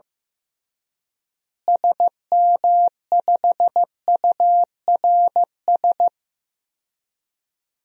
Morse code, heard as a keyed tone, about 15 wpm, 700 Hz. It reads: SM5URS
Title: DXCC-100 Call Signs - 6 Characters - Repeated 3x - 15wpm